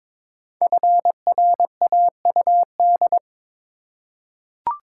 Morse code, heard as a keyed tone, about 22 wpm, 700 Hz. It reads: FRAUD
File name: 5 Letter Words - First 3 Letters Given - 22wpm